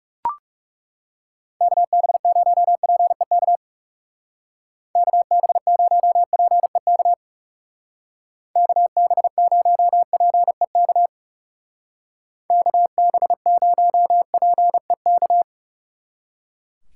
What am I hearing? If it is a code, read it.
Morse code, heard as a keyed tone, about 35 wpm, 700 Hz. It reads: KB0PEK KB0PEK KB0PEK KB0PEK
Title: Speed Racing - USA Call Signs 30wpm